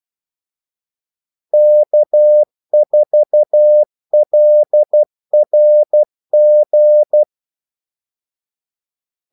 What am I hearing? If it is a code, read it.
Morse code, heard as a keyed tone, about 12 wpm, 600 Hz. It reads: K4LRG
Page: 2 ekstra ordmellemrum mellem kaldesignalerne